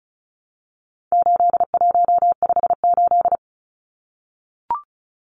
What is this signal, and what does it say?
Morse code, heard as a keyed tone, about 34 wpm, 700 Hz. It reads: 8158
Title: Numbers - 4-Digit 35wpm